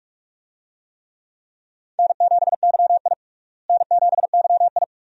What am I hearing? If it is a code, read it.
Morse code, heard as a keyed tone, about 47 wpm, 700 Hz. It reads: N7YI N7YI
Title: USA Call Signs - 4 Characters - Repeated 2x - 45wpm